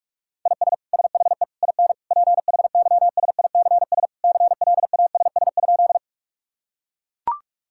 Morse code, heard as a keyed tone, about 41 wpm, 700 Hz. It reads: IS SHE IN PHYSICS CLASS?
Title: Sentences from Top 1000 Words - Review Part 5 - 45wpm